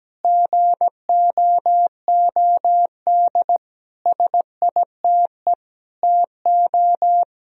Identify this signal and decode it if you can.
Morse code, heard as a keyed tone, about 17 wpm, 700 Hz. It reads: GOOD SITE TO
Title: Sentences from Top 1000 Words - Review Part 4 - 17wpm